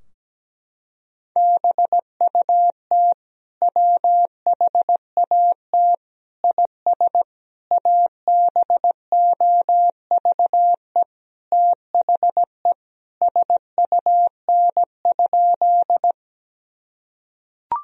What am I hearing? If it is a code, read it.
Morse code, heard as a keyed tone, about 17 wpm, 700 Hz. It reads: BUT WHAT IS ABOVE THE SUN?
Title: Sentences from Top 1000 Words - Review Part 2 - 17wpm